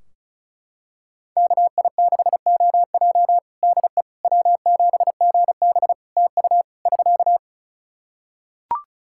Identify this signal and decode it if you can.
Morse code, heard as a keyed tone, about 34 wpm, 700 Hz. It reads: KI6OJ DE W7GB TU <SK>
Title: Practice QSO - Exchange 4 - Ending - 35wpm